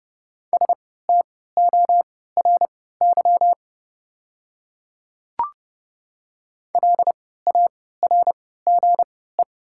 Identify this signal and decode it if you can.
Morse code, heard as a keyed tone, about 31 wpm, spaced as for 17 wpm, 700 Hz. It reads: STORY LARGE
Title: 3x Character Spacing - Top 500 Words - 30wpm